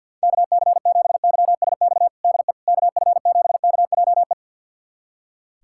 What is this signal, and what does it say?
Morse code, heard as a keyed tone, about 49 wpm, 700 Hz. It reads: KC7YSX DE KF7KJE
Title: USA Callsign DE Callsign - 6 Characters - 50wpm